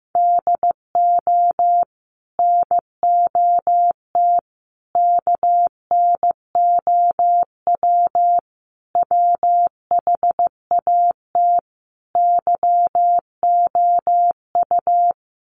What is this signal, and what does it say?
Morse code, heard as a keyed tone, about 15 wpm, 700 Hz. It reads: DO NOT KNOW WHAT YOU
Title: Sets of 5 Words 15wpm